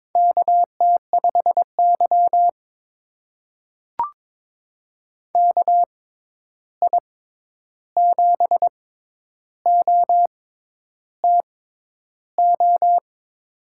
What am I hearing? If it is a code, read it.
Morse code, heard as a keyed tone, about 22 wpm, spaced as for 10 wpm, 700 Hz. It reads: KT5Y KI7OTO
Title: USA Call Signs - 6x Character Spacing - 22wpm